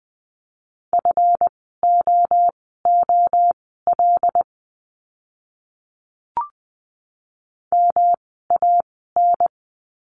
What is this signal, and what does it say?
Morse code, heard as a keyed tone, about 20 wpm, spaced as for 14 wpm, 700 Hz. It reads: FOOL MAN
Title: 139 Lesson 23f - Letter f - Words 20wpm